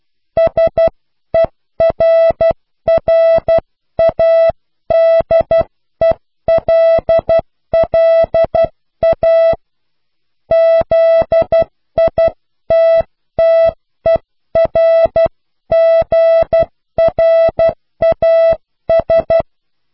Morse code, heard as a keyed tone, about 12 wpm, 660 Hz. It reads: SERRADELLA ZITTERGRAS
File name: cw / pseudo worden(10)